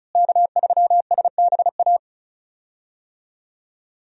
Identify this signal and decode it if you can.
Morse code, heard as a keyed tone, about 36 wpm, 700 Hz. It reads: K3SBA
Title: USA Call Signs - 5 Characters - Repeated 2x - 35wpm